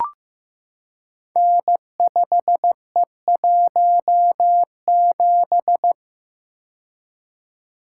Morse code, heard as a keyed tone, about 15 wpm, 700 Hz. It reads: N5E17
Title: Code Groups - 5 Letter:Numbers - 15wpm